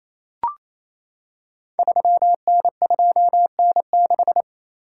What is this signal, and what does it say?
Morse code, heard as a keyed tone, about 27 wpm, 700 Hz. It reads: 3N2N6
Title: Code Groups - 5 Letter:Numbers - 28wpm